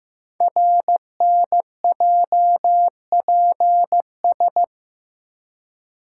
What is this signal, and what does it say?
Morse code, heard as a keyed tone, about 15 wpm, 700 Hz. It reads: RNJPS
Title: Code Groups - 5 Letters - 15wpm